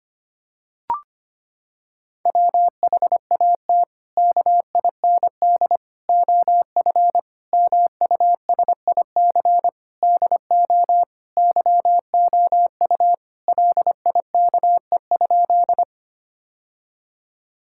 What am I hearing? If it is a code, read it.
Morse code, heard as a keyed tone, about 25 wpm, 700 Hz. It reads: WHAT KIND OF MUSIC DO YOU LIKE?
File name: practice / 25wpm / Sentences with 3-Word Phrases - 25wpm